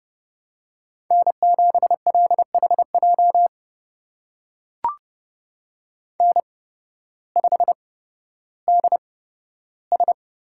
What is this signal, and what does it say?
Morse code, heard as a keyed tone, about 30 wpm, spaced as for 14 wpm, 700 Hz. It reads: N7LHJ N5DS
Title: USA Call Signs - 5 Characters - 8x Character Spacing - 30wpm